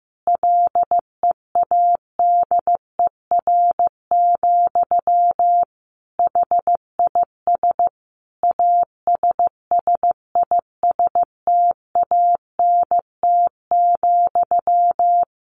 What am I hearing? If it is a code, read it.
Morse code, heard as a keyed tone, about 15 wpm, 700 Hz. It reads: LEADER, HIS ASSISTANT,
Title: A Sound Of Thunder 15wpm